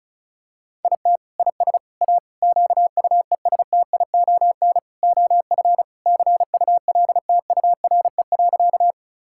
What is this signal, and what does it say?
Morse code, heard as a keyed tone, about 36 wpm, 700 Hz. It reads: IT IS A QUESTION OF CULTURE.
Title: Sentences from Top 1000 Words - Review Part 6 - 35wpm